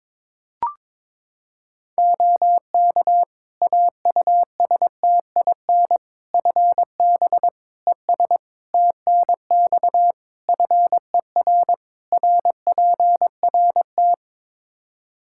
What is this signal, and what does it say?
Morse code, heard as a keyed tone, about 22 wpm, 700 Hz. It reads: OK AUSTIN FB ES TNX FER RPRT
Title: Practice QSO - Exchange 2 - Info - 22wpm